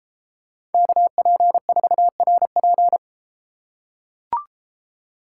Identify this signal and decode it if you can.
Morse code, heard as a keyed tone, about 32 wpm, 700 Hz. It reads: KP4RP